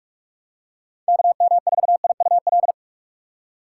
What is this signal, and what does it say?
Morse code, heard as a keyed tone, about 46 wpm, 700 Hz. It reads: KM4IUL
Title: USA Call Signs - 2x Character Spacing - 45wpm